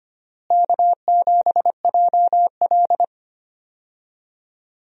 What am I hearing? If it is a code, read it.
Morse code, heard as a keyed tone, about 25 wpm, 700 Hz. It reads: K7JL
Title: USA Call Signs - 4 Characters - Repeated 2x - 25wpm